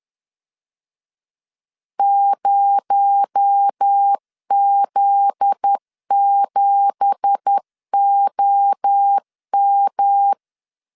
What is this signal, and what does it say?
Morse code, heard as a keyed tone, about 10 wpm, 800 Hz. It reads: 0Z7OM
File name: CW / DET / 1 9 1500C S B 10wpm 4439